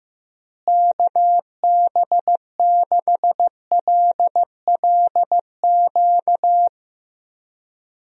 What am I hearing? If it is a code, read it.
Morse code, heard as a keyed tone, about 15 wpm, 700 Hz. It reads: KB6LLQ
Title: USA Call Signs - 6 Characters - Repeated 3x - 15wpm